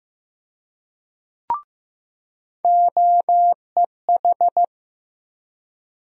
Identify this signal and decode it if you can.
Morse code, heard as a keyed tone, about 15 wpm, 700 Hz. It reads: OEH
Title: USA Call Sign Postfixes - 3 Letters 15wpm